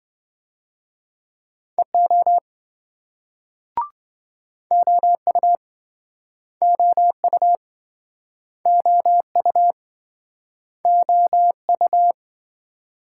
Morse code, heard as a keyed tone, about 28 wpm, 700 Hz. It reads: EO OU OU OU OU